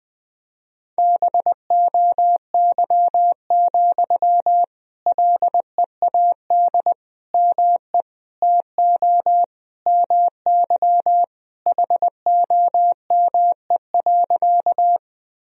Morse code, heard as a keyed tone, about 20 wpm, 700 Hz. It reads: BOY, LEAD ME TO MY HOME.
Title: Sentences from Top 1000 Words - Review Part 2 - 20wpm